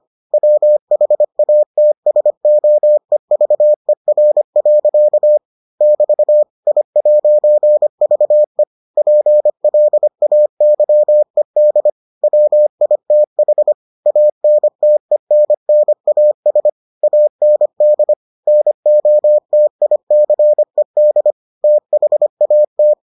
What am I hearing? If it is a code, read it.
Morse code, heard as a keyed tone, about 25 wpm, 600 Hz. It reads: WHATSOEVER. <BT> I'VE PLAYED WITH ANTENNAS AND NOTICED THAT